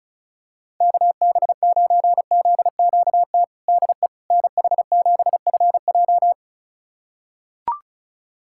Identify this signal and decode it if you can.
Morse code, heard as a keyed tone, about 34 wpm, 700 Hz. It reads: KB9ZQT DE NH7FJ
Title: USA Callsign DE Callsign 35wpm